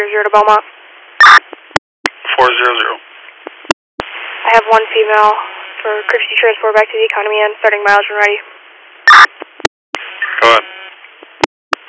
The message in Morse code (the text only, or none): none